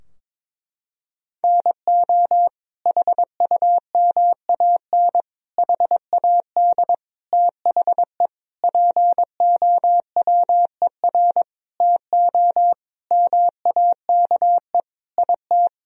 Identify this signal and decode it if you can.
Morse code, heard as a keyed tone, about 22 wpm, 700 Hz. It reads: NO HUMAN HAD THE POWER TO MAKE IT
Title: Sentences from Top 1000 Words - Review Part 3 - 22wpm